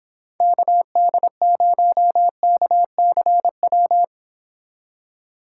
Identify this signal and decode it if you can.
Morse code, heard as a keyed tone, about 26 wpm, 700 Hz. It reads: KD0KCW